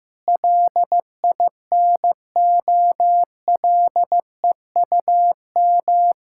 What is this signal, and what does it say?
Morse code, heard as a keyed tone, about 14 wpm, 700 Hz. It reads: LINOLEUM
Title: Pangram Challenge v2 - 15wpm